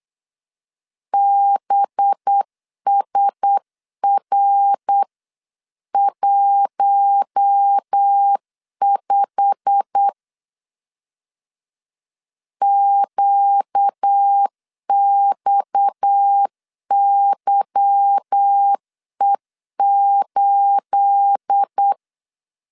Morse code, heard as a keyed tone, about 8 wpm, 800 Hz. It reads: BSR15 QXYE8
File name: CW / DET / 1 9 1500C S A 8wpm 5637